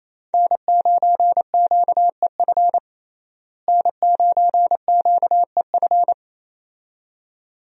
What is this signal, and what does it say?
Morse code, heard as a keyed tone, about 28 wpm, 700 Hz. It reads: N9QEF N9QEF
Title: USA Call Signs - 5 Characters - Repeated 2x - 28wpm